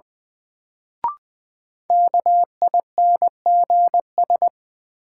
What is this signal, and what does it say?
Morse code, heard as a keyed tone, about 20 wpm, 700 Hz. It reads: KINGS
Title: Sentences from Top 1000 Words - Review Part 6 - New Word Drill - 20wpm